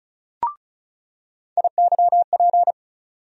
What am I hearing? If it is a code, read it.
Morse code, heard as a keyed tone, about 36 wpm, 700 Hz. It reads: IYP
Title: USA Call Sign Postfixes - 3 Letters 35wpm